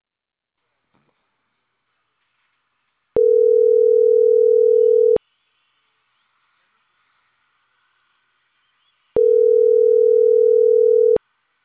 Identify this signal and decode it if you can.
Ringback tone (North America), 440 + 480 Hz.